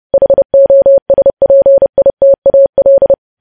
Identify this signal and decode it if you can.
Morse code, heard as a keyed tone, about 30 wpm, 550 Hz. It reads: HOSPITAL
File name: podcast / News headlines-1x-30x30